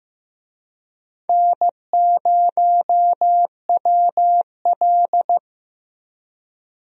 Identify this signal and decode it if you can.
Morse code, heard as a keyed tone, about 15 wpm, 700 Hz. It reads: N0WL
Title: USA Call Signs - 4 Characters - Repeated 2x - 15wpm